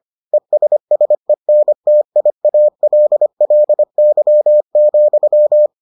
Morse code, heard as a keyed tone, about 25 wpm, 600 Hz. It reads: ESSENTIALLY,